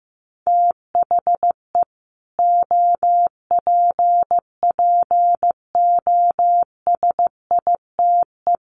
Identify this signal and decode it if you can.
Morse code, heard as a keyed tone, about 15 wpm, 700 Hz. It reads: THE OPPOSITE
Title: Sets of 5 Words Encore 15wpm